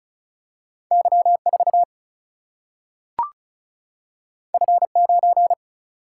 Morse code, heard as a keyed tone, about 36 wpm, 700 Hz. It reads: Y4 F9
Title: Single-Letter Single-Number 35wpm